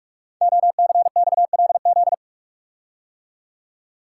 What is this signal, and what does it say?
Morse code, heard as a keyed tone, about 44 wpm, 700 Hz. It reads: OCXLZ
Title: Code Groups - 5 Letters - 45wpm